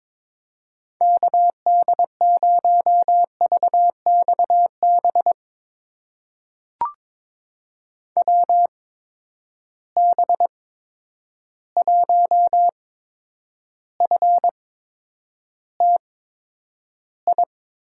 Morse code, heard as a keyed tone, about 22 wpm, spaced as for 9 wpm, 700 Hz. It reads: KD0VXB WB1FTI
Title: USA Call Signs - 6 Characters - 8x Character Spacing - 22wpm